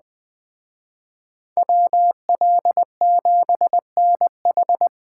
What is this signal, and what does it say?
Morse code, heard as a keyed tone, about 20 wpm, 700 Hz. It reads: WL7NH